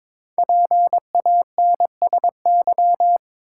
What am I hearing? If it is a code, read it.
Morse code, heard as a keyed tone, about 22 wpm, 690 Hz. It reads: PANSY
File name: Difficult Words with Hints - 22wpm